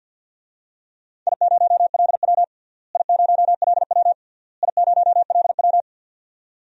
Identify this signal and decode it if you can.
Morse code, heard as a keyed tone, about 52 wpm, 700 Hz. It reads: I0LW I0LW I0LW
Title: DXCC-100 Call Signs - 4 Characters - Repeated 3x - 50wpm